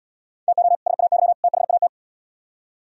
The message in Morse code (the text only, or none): TO SAY THAT